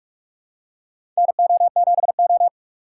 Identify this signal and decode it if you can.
Morse code, heard as a keyed tone, about 44 wpm, 700 Hz. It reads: NO7O
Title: USA Call Signs - 4 Characters - 4x Character Spacing - 45wpm